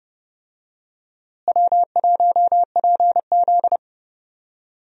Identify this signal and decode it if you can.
Morse code, heard as a keyed tone, about 30 wpm, 700 Hz. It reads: W1PZ
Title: USA Call Signs - 4 Characters - Repeated 2x - 30wpm